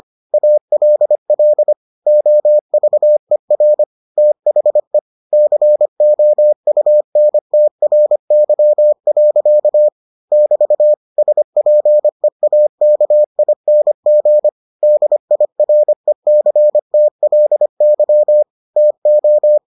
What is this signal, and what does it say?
Morse code, heard as a keyed tone, about 25 wpm, 600 Hz. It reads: ALL OVER THE COUNTRY. <BT> SPEAKING DIRECTLY TO